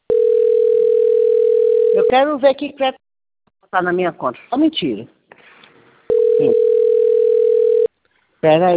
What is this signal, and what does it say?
Ringback tone (North America), 440 + 480 Hz, with speech.
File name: out- - - - - .20521